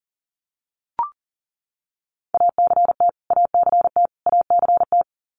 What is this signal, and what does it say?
Morse code, heard as a keyed tone, about 37 wpm, 700 Hz. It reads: ACT ACT ACT
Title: Top 300 Words Repeated 40wpm